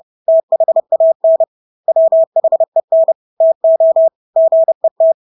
Morse code, heard as a keyed tone, about 30 wpm, 650 Hz. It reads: THAN WHEN TO GET
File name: Engelske ord 30-30 wpm